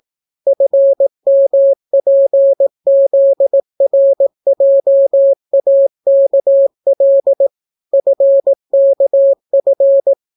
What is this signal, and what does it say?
Morse code, heard as a keyed tone, about 18 wpm, 550 Hz. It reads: FMPZRJAKL FKF